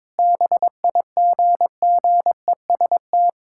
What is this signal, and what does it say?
Morse code, heard as a keyed tone, about 22 wpm, 700 Hz. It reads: BIGGEST
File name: practice / 22wpm / Sentences with 4-Word Phrases - 22wpm